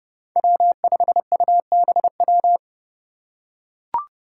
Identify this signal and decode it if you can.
Morse code, heard as a keyed tone, about 29 wpm, 700 Hz. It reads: W5UBW
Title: Five Challenging Letter-Number Combinations 30wpm